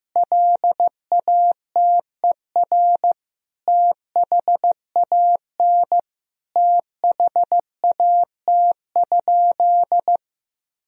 Morse code, heard as a keyed tone, about 15 wpm, 700 Hz. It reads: LATER THAN THAT?
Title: Sentences from Top 1000 Words - Review Part 4 - 15wpm